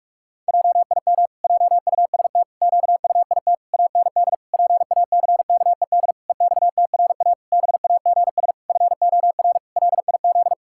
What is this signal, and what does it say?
Morse code, heard as a keyed tone, about 46 wpm, 700 Hz. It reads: JIM JUST QUIT AND PACKED EXTRA BAGS FOR LIZ
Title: Pangram Challenge v2 - 45wpm